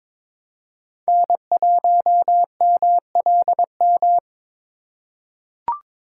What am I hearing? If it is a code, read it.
Morse code, heard as a keyed tone, about 22 wpm, 700 Hz. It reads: N1MLM
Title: USA Call Signs - 5 Characters - 4x Character Spacing - 22wpm